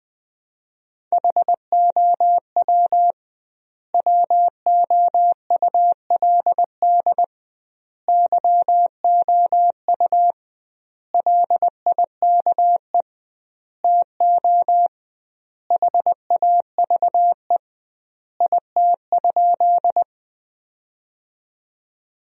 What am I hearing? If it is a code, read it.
Morse code, heard as a keyed tone, about 20 wpm, 700 Hz. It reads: HOW WOULD YOU LIKE TO HAVE IT?
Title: 2x Word-Spacing - Sentences from Top 100 Words - 20wpm